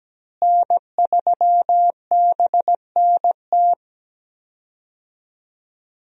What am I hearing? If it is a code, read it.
Morse code, heard as a keyed tone, about 17 wpm, 700 Hz. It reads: N3BNT
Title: USA Call Signs - 5 Characters - Repeated 2x - 17wpm